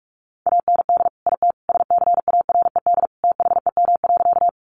Morse code, heard as a keyed tone, about 44 wpm, 700 Hz. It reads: AND IT SCARED THEM.
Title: Sentences from Top 700 Words 45wpm